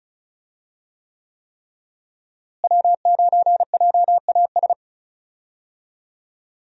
Morse code, heard as a keyed tone, about 36 wpm, 700 Hz. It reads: W9JAS
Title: USA Call Signs - 5 Characters - Repeated 3x - 35wpm